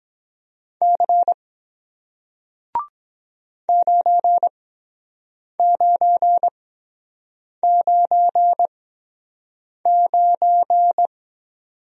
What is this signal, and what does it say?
Morse code, heard as a keyed tone, about 23 wpm, spaced as for 7 wpm, 700 Hz. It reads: C 9999